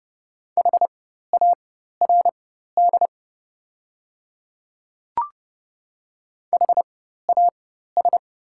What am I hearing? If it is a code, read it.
Morse code, heard as a keyed tone, about 30 wpm, spaced as for 14 wpm, 700 Hz. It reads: HARD HAS
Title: 4x Character Spacing - Top 300 Words - 30wpm